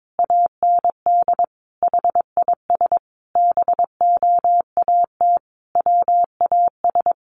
Morse code, heard as a keyed tone, about 22 wpm, 700 Hz. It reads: AND HIS BOAT WAS
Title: Sentences from Top 1400 Words 22wpm